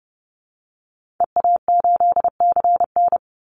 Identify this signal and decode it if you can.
Morse code, heard as a keyed tone, about 30 wpm, 700 Hz. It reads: EA8CN
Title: Call Signs Repeated 2x - 30wpm